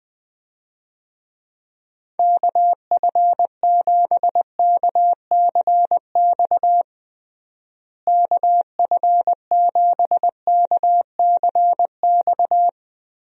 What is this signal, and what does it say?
Morse code, heard as a keyed tone, about 21 wpm, 700 Hz. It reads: KF7KCX KF7KCX
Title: USA Call Signs - 6 Characters - Repeated 3x - 20wpm